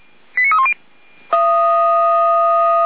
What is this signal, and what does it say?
Five-tone selective call (ZVEI), heard as quick runs of five tones, tones of about 70 ms. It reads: 89310